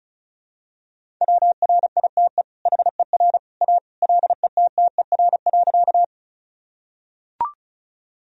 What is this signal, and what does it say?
Morse code, heard as a keyed tone, about 36 wpm, 700 Hz. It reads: WRITE HER A LETTER.
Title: Sentences from Top 1000 Words - Review Part 1 - 35wpm